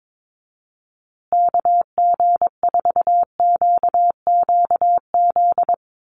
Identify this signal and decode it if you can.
Morse code, heard as a keyed tone, about 22 wpm, 700 Hz. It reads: KG4QQZ
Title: USA Call Signs Repeated 2x 22wpm